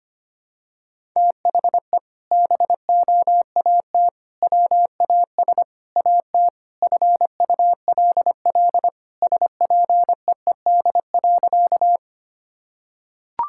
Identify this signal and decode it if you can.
Morse code, heard as a keyed tone, about 25 wpm, 700 Hz. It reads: THE BOAT WAS AT FULL SPEED.
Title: Sentences from Top 1000 Words - Review Part 5 - 25wpm